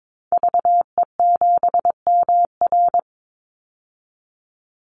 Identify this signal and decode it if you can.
Morse code, heard as a keyed tone, about 22 wpm, 700 Hz. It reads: VE7MR
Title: Call Signs Repeated 3x - 22wpm